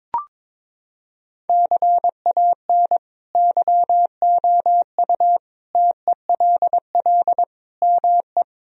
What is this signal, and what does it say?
Morse code, heard as a keyed tone, about 22 wpm, 700 Hz. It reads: CAN YOU TELL ME
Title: Sentences from Top 1000 Words - Review Part 2 - 22wpm